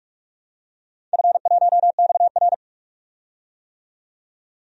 Morse code, heard as a keyed tone, about 46 wpm, 700 Hz. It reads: F1XR
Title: DXCC-100 Call Signs - 4 Characters - 45wpm